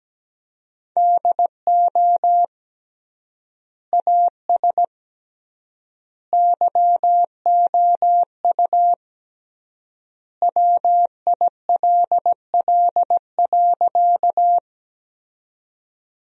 Morse code, heard as a keyed tone, about 17 wpm, 700 Hz. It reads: DO AS YOU WILL.
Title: 3x Word-Spacing - Sentences from Top 100 Words - 17wpm